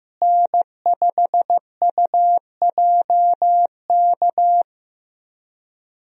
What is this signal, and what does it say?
Morse code, heard as a keyed tone, about 15 wpm, 700 Hz. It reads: N5UJK
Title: USA Call Signs - 5 Characters - Repeated 3x - 15wpm